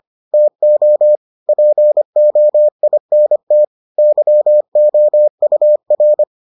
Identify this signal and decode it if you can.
Morse code, heard as a keyed tone, about 25 wpm, 600 Hz. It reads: TO POINT YOUR